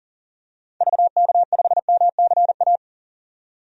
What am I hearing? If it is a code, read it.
Morse code, heard as a keyed tone, about 39 wpm, 700 Hz. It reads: VK5MCA
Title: DXCC-100 Call Signs - 6 Characters - 40wpm